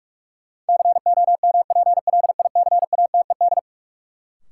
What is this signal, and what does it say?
Morse code, heard as a keyed tone, about 46 wpm, 700 Hz. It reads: COMPLICATED
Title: Sentences from Top 1200 Words - New Word Drill - 45wpm